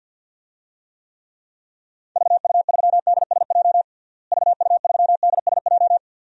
Morse code, heard as a keyed tone, about 52 wpm, 700 Hz. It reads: VU3DSJ VU3DSJ
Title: DXCC-100 Call Signs - 6 Characters - Repeated 3x - 50wpm